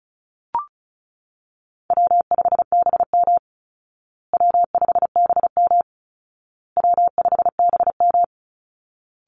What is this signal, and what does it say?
Morse code, heard as a keyed tone, about 34 wpm, 700 Hz. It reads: W5BM W5BM W5BM
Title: Easy Call Signs Repeated 3x - 35wpm